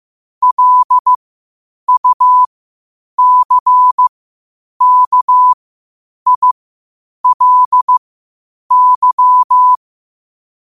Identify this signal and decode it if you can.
Morse code, heard as a keyed tone, about 15 wpm, spaced as for 8 wpm, 1000 Hz. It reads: LUCKILY
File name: qotdfiles / 8wpm